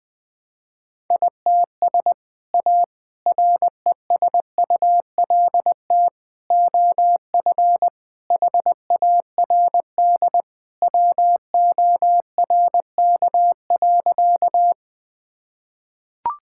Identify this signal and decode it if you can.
Morse code, heard as a keyed tone, about 20 wpm, 700 Hz. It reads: ITS A RESULT OF HARD WORK.
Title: Sentences with 3-Word Phrases - Spoken Beginning - 20wpm